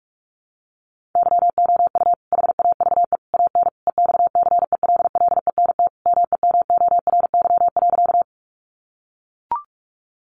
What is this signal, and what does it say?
Morse code, heard as a keyed tone, about 44 wpm, 700 Hz. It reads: YOU HAVE AN EXCELLENT MEMORY.
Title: Sentences from Top 1300 Words 45wpm